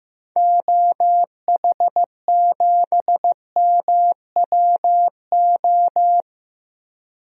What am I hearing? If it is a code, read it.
Morse code, heard as a keyed tone, about 15 wpm, 700 Hz. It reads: OH7MWO
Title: DXCC-100 Call Signs - 6 Characters - 15wpm